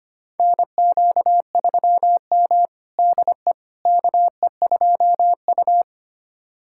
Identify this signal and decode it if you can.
Morse code, heard as a keyed tone, about 25 wpm, 700 Hz. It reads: NQ3M DE KE2U
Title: USA Callsign DE Callsign - 4 Characters - 25wpm